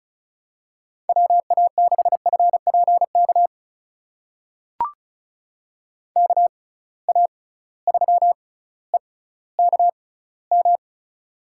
Morse code, heard as a keyed tone, about 36 wpm, spaced as for 18 wpm, 700 Hz. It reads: WA6FPK KA3EKM
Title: USA Call Signs - 6 Characters - 6x Character Spacing - 35wpm